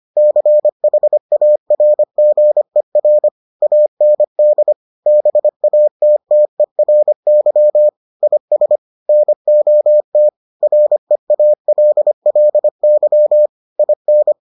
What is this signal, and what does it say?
Morse code, heard as a keyed tone, about 25 wpm, 600 Hz. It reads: CHARGER AND BATTERY IS NOT REALLY IN